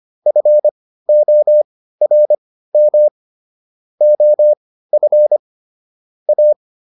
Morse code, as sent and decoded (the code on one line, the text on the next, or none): ..-. --- .-. -- / --- ..-. / .-
FORM OF A